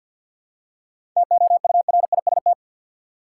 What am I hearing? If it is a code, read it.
Morse code, heard as a keyed tone, about 52 wpm, 700 Hz. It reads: TOURIST
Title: Seven Letter Words - 50wpm